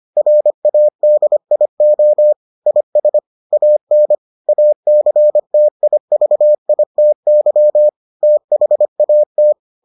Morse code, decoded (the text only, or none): RADIO IS AN ACTIVITY THAT